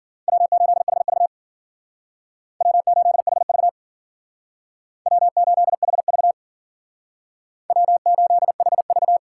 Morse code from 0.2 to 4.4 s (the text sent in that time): W8HV W8HV